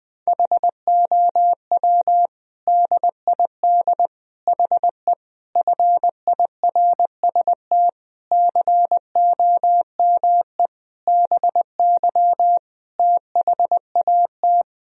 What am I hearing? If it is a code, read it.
Morse code, heard as a keyed tone, about 20 wpm, 700 Hz. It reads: HOW DID HE FIRST COME BY THAT
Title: Sentences from Top 1000 Words - Review Part 2 - 20wpm